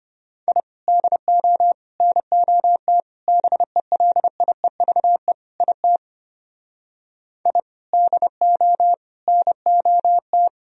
Morse code, heard as a keyed tone, about 32 wpm, 700 Hz. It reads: I DO NOT BELIEVE IT I DO NOT